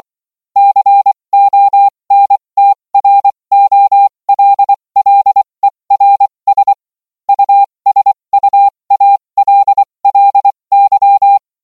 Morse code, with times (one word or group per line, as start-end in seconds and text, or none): 0.6-6.7 CONTROLLERS
7.3-11.4 USUALLY